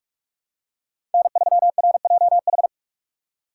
Morse code, heard as a keyed tone, about 45 wpm, 700 Hz. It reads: N3RJH
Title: USA Call Signs - 5 Characters - 2x Character Spacing - 45wpm